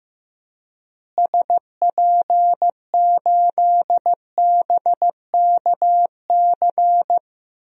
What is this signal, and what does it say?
Morse code, heard as a keyed tone, about 15 wpm, 700 Hz. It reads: SP8BKC
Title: DXCC-100 Call Signs - 6 Characters - Repeated 2x - 15wpm